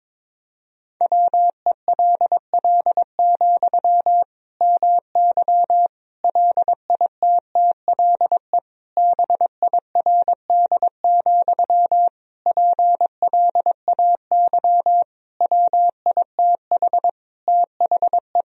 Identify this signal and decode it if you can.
Morse code, heard as a keyed tone, about 22 wpm, 700 Hz. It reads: WELL, MY LITTLE BIRD, PLAY WITH THE
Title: Sentences from Top 1000 Words - Review Part 2 - 22wpm